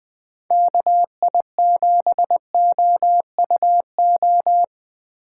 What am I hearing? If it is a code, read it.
Morse code, heard as a keyed tone, about 21 wpm, 700 Hz. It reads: KI7OUO
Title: USA Call Signs - 2x Character Spacing - 20wpm